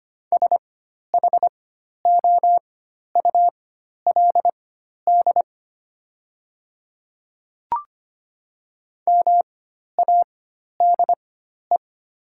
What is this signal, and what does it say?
Morse code, heard as a keyed tone, about 25 wpm, spaced as for 12 wpm, 690 Hz. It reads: SHOULD MADE
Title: 4x Character Spacing - Top 300 Words - 25wpm